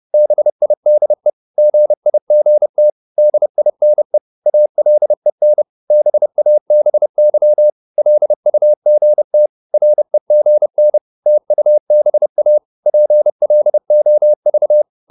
Morse code, heard as a keyed tone, about 30 wpm, 600 Hz. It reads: BIDE GIGT DINE ALEN BABY LUGT REGN TUBA PLOV
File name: Korte ord DK 30wpm